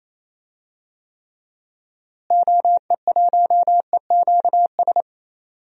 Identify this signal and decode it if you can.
Morse code, heard as a keyed tone, about 28 wpm, 700 Hz. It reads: OE1EQS
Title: DXCC-100 Call Signs - 6 Characters - Repeated 3x - 28wpm